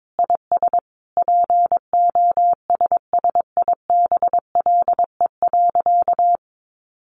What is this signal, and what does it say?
Morse code, heard as a keyed tone, about 22 wpm, 700 Hz. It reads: IS POSSIBLE.
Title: Sentences from Top 900 Words 22wpm